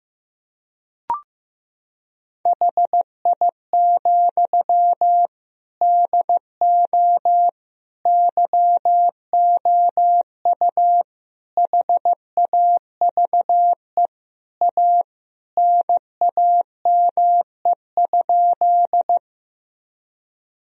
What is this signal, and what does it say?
Morse code, heard as a keyed tone, about 15 wpm, 700 Hz. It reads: HI, DO YOU HAVE A NAME?
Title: Sentences from Top 1000 Words - Review Part 3 - 15wpm